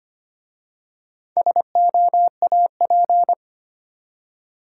Morse code, heard as a keyed tone, about 25 wpm, 700 Hz. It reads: SOAP
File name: practice / 25wpm / Easy Words with Hints - 25wpm